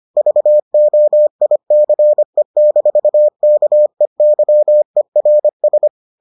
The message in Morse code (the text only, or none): VOICE-KEYERS